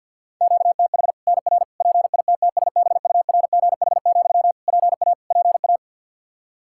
Morse code, heard as a keyed tone, about 51 wpm, 700 Hz. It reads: QTH NR PITTSBURGH, PA PA
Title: Practice QSO - Exchange 1 - Beginning - 50wpm